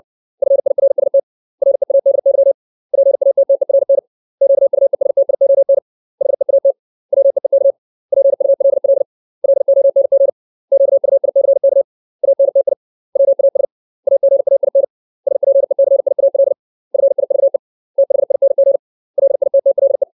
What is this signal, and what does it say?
Morse code, heard as a keyed tone, about 60 wpm, 550 Hz. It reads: FIRST READY PATTERN QUESTION HEAT WEEK PULL LONG QUICK ANTI WAS AGAIN SPECIAL FIVE THING LITTLE
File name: 632 words / 632 Words 60WPM